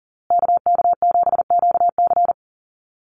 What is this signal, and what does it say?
Morse code, heard as a keyed tone, about 40 wpm, 700 Hz. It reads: KK7QC
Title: USA Call Signs Repeated 2x 40wpm